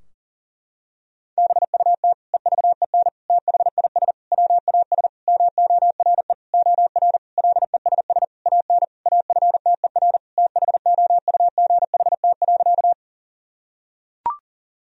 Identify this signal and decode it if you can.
Morse code, heard as a keyed tone, about 40 wpm, 700 Hz. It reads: BUT EVEN THIS WAS MORE OR LESS AN AFTER THOUGHT.
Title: Sentences from Top 1000 Words - Review Part 2 - 40wpm